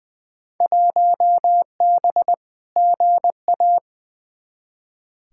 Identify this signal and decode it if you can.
Morse code, heard as a keyed tone, about 21 wpm, 700 Hz. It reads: 1B GA
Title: Field Day Exchanges Encore - 20wpm